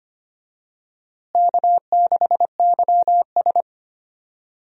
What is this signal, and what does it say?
Morse code, heard as a keyed tone, about 25 wpm, 700 Hz. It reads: K6YS
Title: USA Call Signs - 4 Characters - Repeated 2x - 25wpm